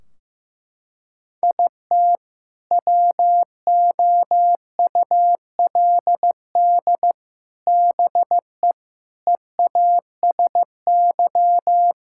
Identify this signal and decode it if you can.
Morse code, heard as a keyed tone, about 15 wpm, 700 Hz. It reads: IT WOULD BE EASY
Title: Sentences from Top 1600 Words 15wpm